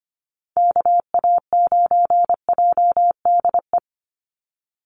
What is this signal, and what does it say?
Morse code, heard as a keyed tone, about 25 wpm, 700 Hz. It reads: KA9JDE
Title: USA Call Signs 25wpm